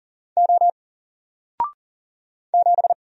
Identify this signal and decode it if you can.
Morse code, heard as a keyed tone, about 40 wpm, 700 Hz. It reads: O 7
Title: Single Letter-Number 40wpm